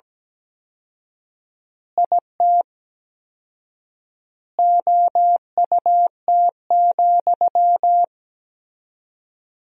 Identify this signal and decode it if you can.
Morse code, heard as a keyed tone, about 17 wpm, 700 Hz. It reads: IT OUT,
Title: 4x Word-Spacing - Sentences from Top 300 Words - 17wpm